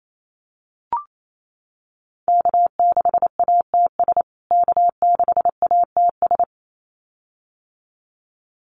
Morse code, heard as a keyed tone, about 28 wpm, 700 Hz. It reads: K6ATS K6ATS
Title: USA Call Signs Repeated 2x 28wpm